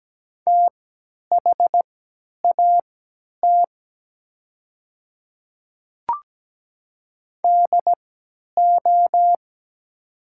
Morse code, heard as a keyed tone, about 17 wpm, spaced as for 10 wpm, 700 Hz. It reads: THAT DO
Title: 3x Character Spacing - Top 100 Words - 17wpm